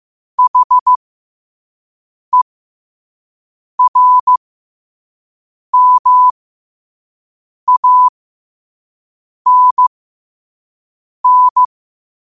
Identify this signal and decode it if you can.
Morse code, heard as a keyed tone, about 15 wpm, spaced as for 5 wpm, 1000 Hz. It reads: HERMANN
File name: qotdfiles / 5wpm